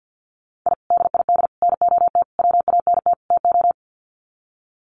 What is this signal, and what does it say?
Morse code, heard as a keyed tone, about 50 wpm, 700 Hz. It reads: I DID NOT WANT TO
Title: Sets of 5 Words Encore 50wpm